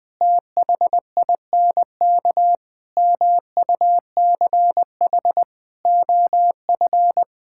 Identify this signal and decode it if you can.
Morse code, heard as a keyed tone, about 21 wpm, 700 Hz. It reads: THINK MUCH OF
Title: Sentences from Top 1000 Words - Review Part 6 - 20wpm